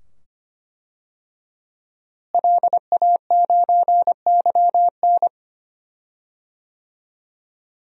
Morse code, heard as a keyed tone, about 25 wpm, 700 Hz. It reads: LA9YN
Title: DXCC-100 Call Signs - 5 Characters - Repeated 2x - 25wpm